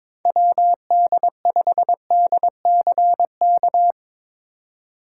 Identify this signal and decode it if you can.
Morse code, heard as a keyed tone, about 22 wpm, 700 Hz. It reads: WD5DCK
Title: USA Call Signs - 6 Characters - 3x Character Spacing - 22wpm